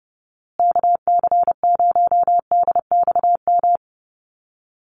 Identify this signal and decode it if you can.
Morse code, heard as a keyed tone, about 30 wpm, 700 Hz. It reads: KC0DXM
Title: USA Call Signs Repeated 2x 30wpm